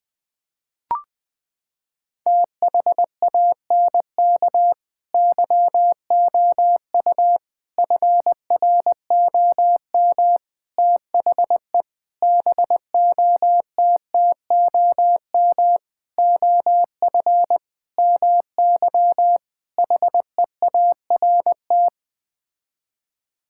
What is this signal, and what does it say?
Morse code, heard as a keyed tone, about 20 wpm, 700 Hz. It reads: THANK YOU FROM THE BOTTOM OF MY HEART
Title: Sets of 8 Words - 20wpm